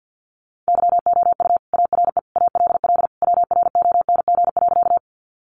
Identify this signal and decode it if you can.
Morse code, heard as a keyed tone, about 50 wpm, 700 Hz. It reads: YOU ARE ALL WRONG.
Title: Sentences from Top 900 Words 50wpm